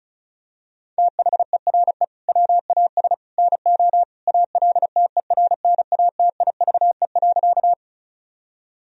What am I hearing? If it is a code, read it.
Morse code, heard as a keyed tone, about 36 wpm, 700 Hz. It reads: THERE WAS NO ALTERNATIVE.
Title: Sentences from Top 1000 Words - Review Part 4 - 35wpm